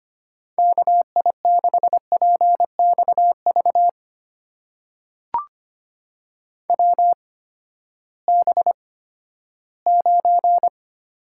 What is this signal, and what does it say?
Morse code, heard as a keyed tone, about 25 wpm, spaced as for 14 wpm, 700 Hz. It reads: KI6PXV WB9
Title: USA Call Signs - 6 Characters - 8x Character Spacing - 25wpm